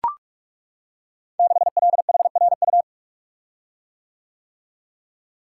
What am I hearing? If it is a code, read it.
Morse code, heard as a keyed tone, about 42 wpm, 700 Hz. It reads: 6LHRU
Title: Code Groups - 5 Letter:Numbers - 45wpm